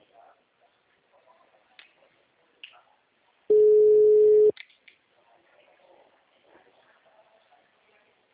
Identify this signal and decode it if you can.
Ringback tone (Europe, 425 Hz).
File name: out- -2025